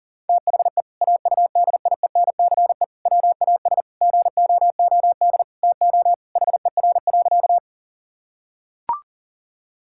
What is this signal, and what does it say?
Morse code, heard as a keyed tone, about 40 wpm, 700 Hz. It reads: THE AUDIENCE WAS GOOD TO HER.
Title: Sentences from Top 1000 Words - Review Part 5 - 40wpm